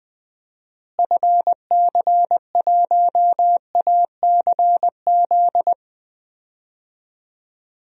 Morse code, heard as a keyed tone, about 21 wpm, 700 Hz. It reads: FC1ACZ
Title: DXCC-100 Call Signs - 6 Characters - Repeated 2x - 20wpm